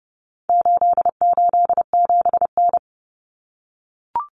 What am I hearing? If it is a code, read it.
Morse code, heard as a keyed tone, about 30 wpm, 700 Hz. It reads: 887N
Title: Common Cut Numbers - 4-Digit 30wpm